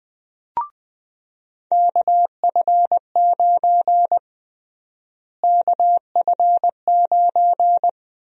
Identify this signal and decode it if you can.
Morse code, heard as a keyed tone, about 20 wpm, 700 Hz. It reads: KF9 KF9
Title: USA Call Sign Prefixes - Repeated 2x - 20wpm